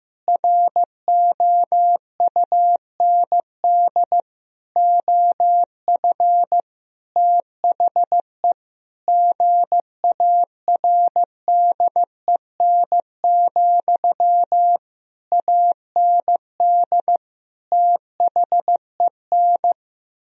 Morse code, heard as a keyed tone, about 15 wpm, 700 Hz. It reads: ROUND OF THE GARDEN, AND THEN
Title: Sentences from Top 1000 Words - Review Part 4 - 15wpm